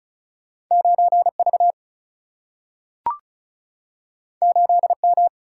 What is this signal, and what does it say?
Morse code, heard as a keyed tone, about 36 wpm, 700 Hz. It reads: 9 V 8 M
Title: Single-Letter Single-Number 35wpm